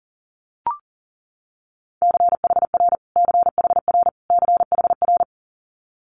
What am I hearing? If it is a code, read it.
Morse code, heard as a keyed tone, about 41 wpm, 700 Hz. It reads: CHR CHR CHR
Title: Multi-letter Phonemes Repeated 3x 40wpm